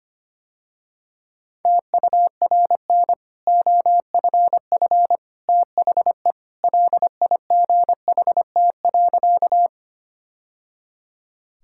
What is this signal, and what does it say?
Morse code, heard as a keyed tone, about 25 wpm, 690 Hz. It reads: TURN OFF THE LIGHT.
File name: Easy Sentences with Hints - 25wpm